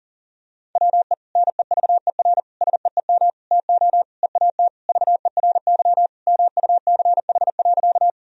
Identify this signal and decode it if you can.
Morse code, heard as a keyed tone, about 40 wpm, 700 Hz. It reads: WE NEVER SEEM TO EAT VERY MUCH.
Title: Sentences from Top 1000 Words - Review Part 1 - 40wpm